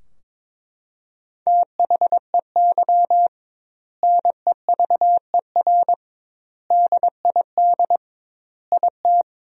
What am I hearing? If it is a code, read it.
Morse code, heard as a keyed tone, about 22 wpm, 700 Hz. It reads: THEY NEVER DID IT
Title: 2x Word-Spacing - Sentences from Top 300 Words - 22wpm